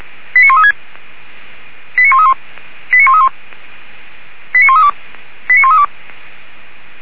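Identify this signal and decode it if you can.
Five-tone selective call (ZVEI), heard as quick runs of five tones, tones of about 70 ms. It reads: 89136 89131 89131 89132 89132
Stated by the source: ZVEI five-tone selective call 89133 received on an SDR, recorded by feuerwehr-orlamuende.org (fire brigade Orlamünde).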